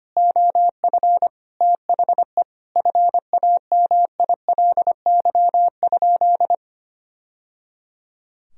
Morse code, heard as a keyed tone, about 25 wpm, 700 Hz. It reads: OF THE FAMILY?
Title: Sentences from Top 1000 Words - Review Part 4 - 25wpm